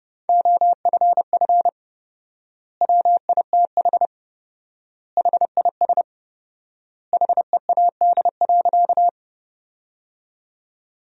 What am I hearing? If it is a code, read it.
Morse code, heard as a keyed tone, about 30 wpm, 700 Hz. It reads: OFF WITH HIS HEAD.
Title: 4x Word-Spacing - Sentences from Top 300 Words - 30wpm